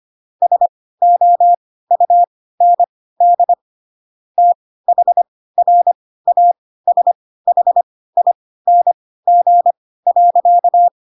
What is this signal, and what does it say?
Morse code, heard as a keyed tone, about 25 wpm, spaced as for 16 wpm, 700 Hz. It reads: SOUND THRASHING.
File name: Pinocchio / Pinocchio ch2 10 25 ik1hge